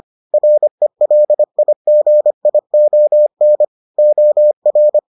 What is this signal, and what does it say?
Morse code, heard as a keyed tone, about 25 wpm, 600 Hz. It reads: RELIGION OR